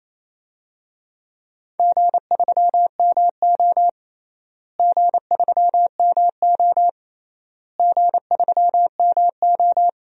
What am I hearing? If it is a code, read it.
Morse code, heard as a keyed tone, about 28 wpm, 700 Hz. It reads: G3MO G3MO G3MO
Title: DXCC-100 Call Signs - 4 Characters - Repeated 3x - 28wpm